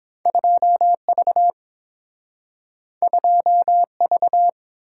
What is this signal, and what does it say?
Morse code, heard as a keyed tone, about 30 wpm, 700 Hz. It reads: 2V 2V